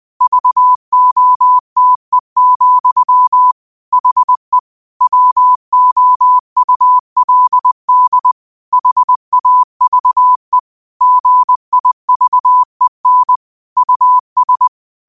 Morse code, heard as a keyed tone, about 20 wpm, 1000 Hz. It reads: VOTE, HE WOULD HAVE GIVEN US
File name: qotdfiles / 20wpm